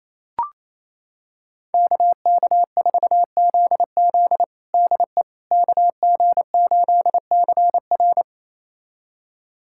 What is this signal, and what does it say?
Morse code, heard as a keyed tone, about 27 wpm, 700 Hz. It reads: KK4ZZ DE KG8CR
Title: USA Callsign DE Callsign - 5 Characters - 28wpm